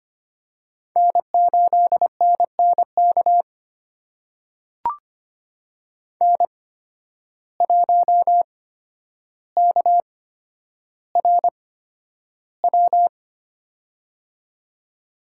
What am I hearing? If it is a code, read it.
Morse code, heard as a keyed tone, about 25 wpm, spaced as for 11 wpm, 700 Hz. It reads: N8NNK N1KRW
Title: USA Call Signs - 5 Characters - 8x Character Spacing - 25wpm